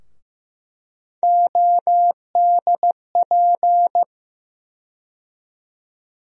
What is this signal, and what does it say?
Morse code, heard as a keyed tone, about 15 wpm, 700 Hz. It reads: ODP
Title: USA Call Sign Postfixes - 3 Letters 15wpm